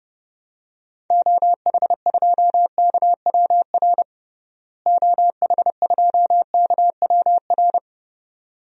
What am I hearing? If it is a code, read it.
Morse code, heard as a keyed tone, about 32 wpm, 700 Hz. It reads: OH2KWR OH2KWR
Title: DXCC-100 Call Signs - 6 Characters - Repeated 2x - 30wpm